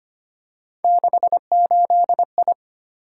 Morse code, heard as a keyed tone, about 25 wpm, 700 Hz. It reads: 68I
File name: Three Letter-Number Combinations - 25wpm